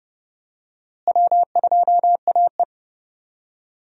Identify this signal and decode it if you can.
Morse code, heard as a keyed tone, about 30 wpm, 700 Hz. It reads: W2AE
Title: USA Call Signs - 4 Characters - Repeated 2x - 30wpm